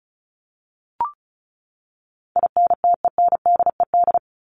Morse code, heard as a keyed tone, about 34 wpm, 700 Hz. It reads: INTENDED
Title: ED Suffix - Top 100 Words with Base Words 35wpm